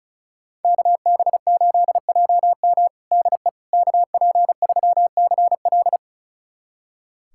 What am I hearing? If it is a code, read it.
Morse code, heard as a keyed tone, about 34 wpm, 700 Hz. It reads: KB8JM DE KP3CL
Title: USA Callsign DE Callsign - 5 Characters - 35wpm